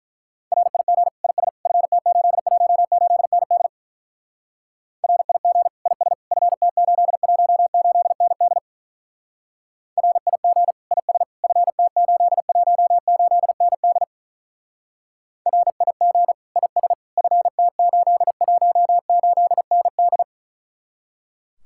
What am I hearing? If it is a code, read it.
Morse code, heard as a keyed tone, about 45 wpm, 700 Hz. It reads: RIG IS FT818ND RIG IS FT818ND RIG IS FT818ND RIG IS FT818ND